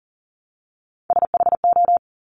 Morse code, heard as a keyed tone, about 40 wpm, 700 Hz. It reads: SHO
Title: Top Three Letter Combinations in Words 40wpm